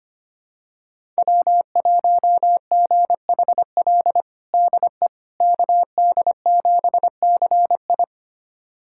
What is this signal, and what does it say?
Morse code, heard as a keyed tone, about 25 wpm, 700 Hz. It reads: W1GHL DE KD7CI
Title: USA Callsign DE Callsign - 5 Characters - 25wpm